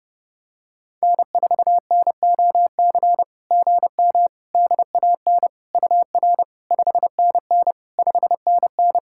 Morse code, heard as a keyed tone, about 30 wpm, 690 Hz. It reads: N4NOC GM DAN UR 5NN 5NN
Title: SOTA QSO Practice 30wpm